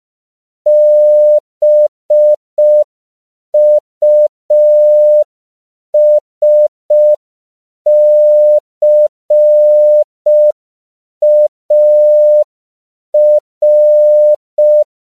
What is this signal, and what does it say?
Morse code, heard as a keyed tone, about 5 wpm, 600 Hz. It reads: BUSCAR
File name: 5PPM